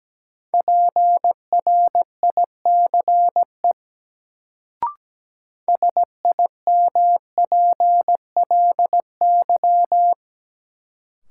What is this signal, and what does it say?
Morse code, heard as a keyed tone, about 17 wpm, 700 Hz. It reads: PRICE SIMPLY
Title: Sentences from Top 600 Words - New Word Drill - 17wpm